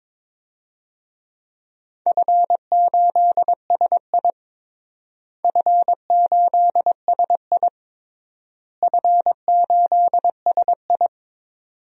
Morse code, heard as a keyed tone, about 22 wpm, 700 Hz. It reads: F8SI F8SI F8SI
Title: DXCC-100 Call Signs - 4 Characters - Repeated 3x - 22wpm